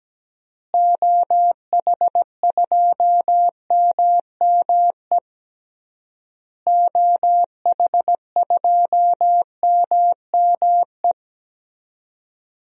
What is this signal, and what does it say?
Morse code, heard as a keyed tone, about 17 wpm, 700 Hz. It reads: OH2MME OH2MME
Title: DXCC-100 Call Signs - 6 Characters - Repeated 3x - 17wpm